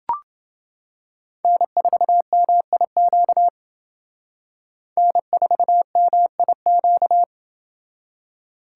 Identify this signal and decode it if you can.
Morse code, heard as a keyed tone, about 29 wpm, 700 Hz. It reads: N4MIQ N4MIQ